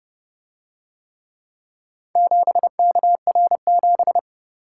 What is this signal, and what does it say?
Morse code, heard as a keyed tone, about 28 wpm, 700 Hz. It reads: 7KR7